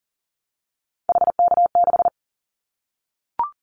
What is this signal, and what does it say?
Morse code, heard as a keyed tone, about 41 wpm, 700 Hz. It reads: HK6
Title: Callsign Prefixes for Top 100 DXCC - 40wpm